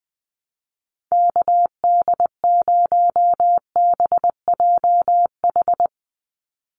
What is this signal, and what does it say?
Morse code, heard as a keyed tone, about 20 wpm, 700 Hz. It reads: KD0BJH
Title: USA Call Signs 20wpm